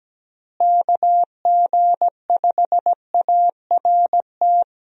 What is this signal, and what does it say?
Morse code, heard as a keyed tone, about 17 wpm, 700 Hz. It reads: KG5ART
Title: USA Call Signs - 6 Characters - Repeated 2x - 17wpm